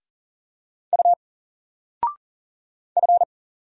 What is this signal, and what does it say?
Morse code, heard as a keyed tone, about 40 wpm, 700 Hz. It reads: U F
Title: Single Letter-Number 40wpm